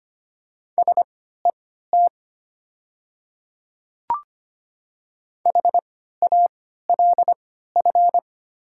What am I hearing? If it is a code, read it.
Morse code, heard as a keyed tone, about 25 wpm, spaced as for 14 wpm, 700 Hz. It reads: SET HALF
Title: 3x Character Spacing - Top 500 Words - 25wpm